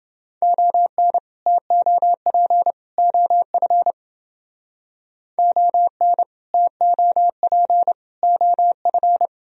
Morse code, heard as a keyed tone, about 30 wpm, 700 Hz. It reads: ON TOP OF ON TOP OF